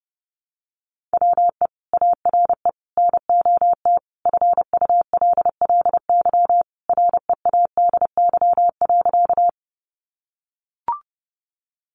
Morse code, heard as a keyed tone, about 29 wpm, 700 Hz. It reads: WE ARE NOT FULLY READY.
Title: Sentences from Top 1300 Words 30wpm